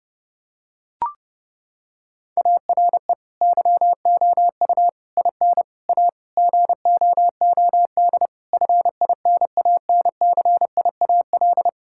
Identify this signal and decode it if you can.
Morse code, heard as a keyed tone, about 32 wpm, 700 Hz. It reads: ARE YOU IN A GOOD FINANCIAL
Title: Sentences from Top 1000 Words - Review Part 5 - 30wpm